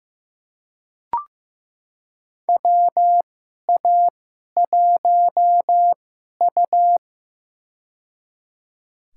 Morse code, heard as a keyed tone, about 15 wpm, spaced as for 11 wpm, 700 Hz. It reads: WA1U
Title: USA Call Signs - 4 Characters - 2x Character Spacing - 15wpm